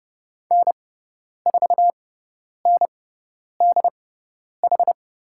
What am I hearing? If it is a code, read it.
Morse code, heard as a keyed tone, about 30 wpm, spaced as for 10 wpm, 700 Hz. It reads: N4NDH
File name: Lesson 16g - Letter c - Call Signs 30wpm